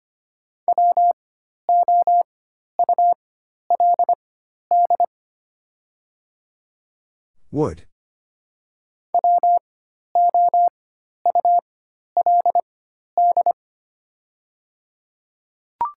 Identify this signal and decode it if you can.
Morse code, heard as a keyed tone, about 25 wpm, spaced as for 12 wpm, 700 Hz. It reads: WOULD WOULD
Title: 4x Character Spacing - Top 100 Words - 25wpm